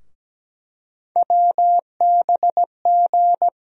Morse code, heard as a keyed tone, about 17 wpm, 700 Hz. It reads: WBG
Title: USA Call Sign Postfixes - 3 Letters 17wpm